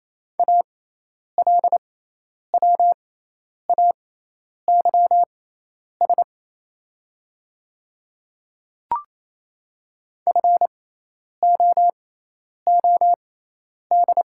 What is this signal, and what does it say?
Morse code, heard as a keyed tone, about 28 wpm, spaced as for 10 wpm, 690 Hz. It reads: ALWAYS FOOD
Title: 6x Character Spacing - Top 300 Words - 28wpm